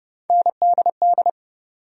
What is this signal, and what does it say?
Morse code, heard as a keyed tone, about 30 wpm, 700 Hz. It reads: NDD
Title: USA Call Sign Postfixes - 3 Letters 30wpm